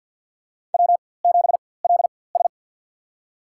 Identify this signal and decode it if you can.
Morse code, heard as a keyed tone, about 47 wpm, spaced as for 23 wpm, 700 Hz. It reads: W7LS
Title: USA Call Signs - 4x Character Spacing - 50wpm